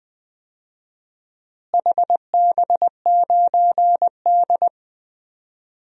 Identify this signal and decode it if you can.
Morse code, heard as a keyed tone, about 20 wpm, 700 Hz. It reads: HB9D
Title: DXCC-100 Call Signs - 4 Characters - Repeated 2x - 20wpm